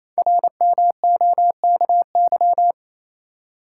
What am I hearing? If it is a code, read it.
Morse code, heard as a keyed tone, about 29 wpm, 700 Hz. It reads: RMOKY
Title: Code Groups - 5 Letters - 28wpm